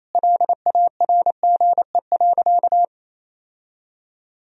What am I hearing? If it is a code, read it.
Morse code, heard as a keyed tone, about 28 wpm, 700 Hz. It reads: LARGE.
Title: Sentences from Top 1000 Words - Review Part 1 - 28wpm